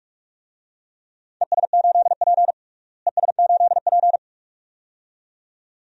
Morse code, heard as a keyed tone, about 46 wpm, 700 Hz. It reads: ES8P ES8P
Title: DXCC-100 Call Signs - 4 Characters - Repeated 2x - 45wpm